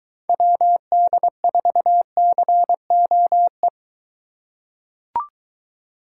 Morse code, heard as a keyed tone, about 20 wpm, 700 Hz. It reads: WD4COE